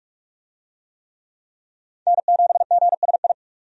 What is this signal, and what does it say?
Morse code, heard as a keyed tone, about 42 wpm, 700 Hz. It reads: N7GSI
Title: USA Call Signs - 5 Characters - 45wpm